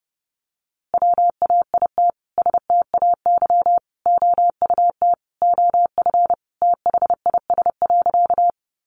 Morse code, heard as a keyed tone, about 30 wpm, 700 Hz. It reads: WAIT STAY OUT OF THIS.
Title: A Sound Of Thunder 30wpm